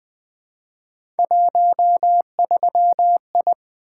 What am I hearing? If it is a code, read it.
Morse code, heard as a keyed tone, about 20 wpm, 700 Hz. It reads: 13I
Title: Three Letter-Number Combinations - 20wpm